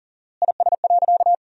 Morse code, heard as a keyed tone, about 37 wpm, 700 Hz. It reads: IS.
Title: 4x Word-Spacing - Sentences from Top 500 Words - 40wpm